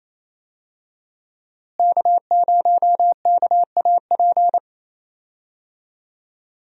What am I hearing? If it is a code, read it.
Morse code, heard as a keyed tone, about 28 wpm, 700 Hz. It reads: K0KAP
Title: USA Call Signs - 5 Characters - Repeated 3x - 28wpm